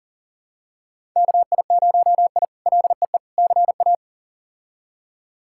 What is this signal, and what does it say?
Morse code, heard as a keyed tone, about 40 wpm, 700 Hz. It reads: KI0I LEE CA
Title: K1USN SST QSO Practice - 40wpm